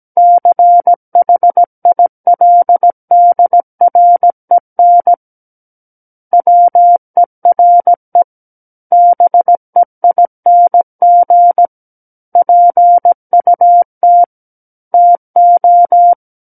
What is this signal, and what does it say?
Morse code, heard as a keyed tone, about 17 wpm, 700 Hz. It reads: CHILDREN WERE BEING PUT TO
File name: Book One / 08-Chapter 8 - 17 WPM